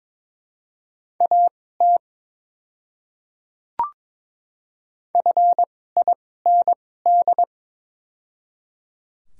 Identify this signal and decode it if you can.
Morse code, heard as a keyed tone, about 22 wpm, spaced as for 16 wpm, 700 Hz. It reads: AT FIND
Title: 2x Character Spacing - Top 100 Words - 22wpm